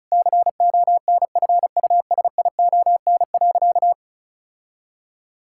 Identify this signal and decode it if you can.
Morse code, heard as a keyed tone, about 36 wpm, 700 Hz. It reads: CONFUSION.
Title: Sentences from Top 1600 Words 35wpm